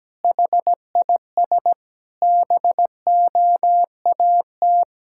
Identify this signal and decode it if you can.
Morse code, heard as a keyed tone, about 17 wpm, 700 Hz. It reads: HIS BOAT
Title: Pangram Challenge v2 - 17wpm